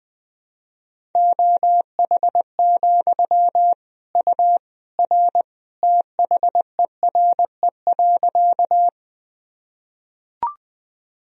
Morse code, heard as a keyed tone, about 21 wpm, 700 Hz. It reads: OH, U R THERE.
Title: Sentences from Top 500 Words and Abbreviations 20wpm